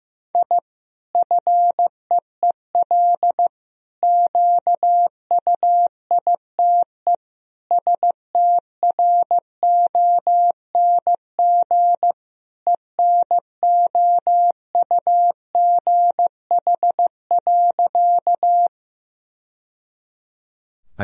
Morse code, heard as a keyed tone, about 16 wpm, 700 Hz. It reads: I FEEL QUITE STRONG ENOUGH.
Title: Sentences from Top 1000 Words - Review Part 3 - 15wpm